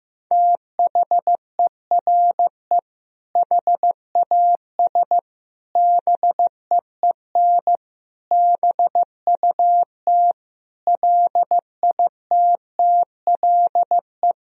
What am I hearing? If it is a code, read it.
Morse code, heard as a keyed tone, about 15 wpm, 700 Hz. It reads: THERE HAS BEEN BUT LITTLE
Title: Sentences from Top 1000 Words - Review Part 1 - 15wpm